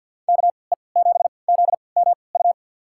Morse code, heard as a keyed tone, about 52 wpm, spaced as for 29 wpm, 700 Hz. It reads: KE7ZMU
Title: USA Call Signs - 3x Character Spacing - 50wpm